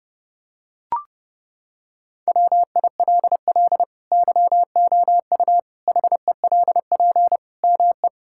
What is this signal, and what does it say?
Morse code, heard as a keyed tone, about 29 wpm, 700 Hz. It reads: WILL YOU HELP ME
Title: Sentences from Top 1000 Words - Review Part 6 - 30wpm